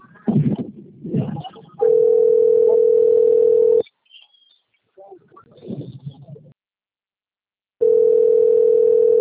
Ringback tone (North America), 440 + 480 Hz.